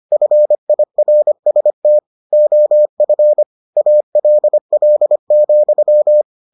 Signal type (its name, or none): Morse code